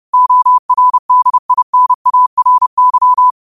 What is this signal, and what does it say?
Morse code, heard as a keyed tone, about 30 wpm, 1000 Hz. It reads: ORDINARY
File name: qotdfiles / 30wpm